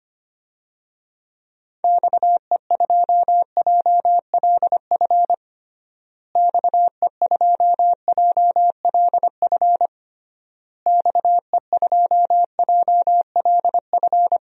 Morse code, heard as a keyed tone, about 25 wpm, 700 Hz. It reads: XE2JLF XE2JLF XE2JLF
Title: DXCC-100 Call Signs - 6 Characters - Repeated 3x - 25wpm